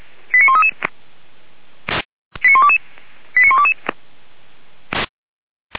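Five-tone selective call (ZVEI), heard as quick runs of five tones, tones of about 70 ms. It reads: 89133 89122 89122
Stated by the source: ZVEI five-tone selective call received by SDR.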